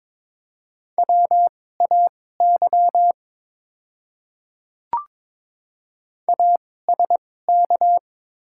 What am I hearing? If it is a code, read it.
Morse code, heard as a keyed tone, about 22 wpm, spaced as for 16 wpm, 700 Hz. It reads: WAY ASK
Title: 2x Character Spacing - Top 300 Words - 22wpm